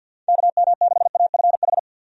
Morse code, heard as a keyed tone, about 51 wpm, 700 Hz. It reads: KM6AFH
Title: USA Call Signs - 6 Characters - Repeated 2x - 50wpm